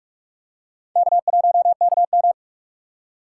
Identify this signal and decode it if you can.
Morse code, heard as a keyed tone, about 44 wpm, 700 Hz. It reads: K1KM